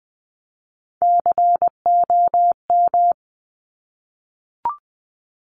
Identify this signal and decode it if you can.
Morse code, heard as a keyed tone, about 20 wpm, 700 Hz. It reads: COM
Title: Top Three Letter Combinations in Words 20wpm